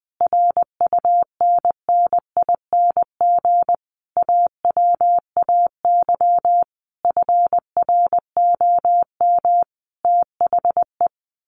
Morse code, heard as a keyed tone, about 20 wpm, 700 Hz. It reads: RUNNING AWAY FROM THE
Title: A Sound Of Thunder 20wpm